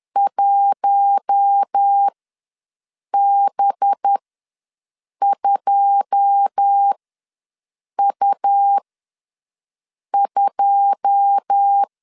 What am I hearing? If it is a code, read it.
Morse code, heard as a keyed tone, about 10 wpm, spaced as for 6 wpm, 800 Hz. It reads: 1B2U2
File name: LEZIONI / 9 2 ws 10wpm 4130